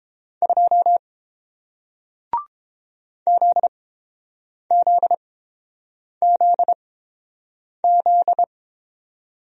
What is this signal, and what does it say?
Morse code, heard as a keyed tone, about 30 wpm, spaced as for 7 wpm, 700 Hz. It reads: 2 ZZZZ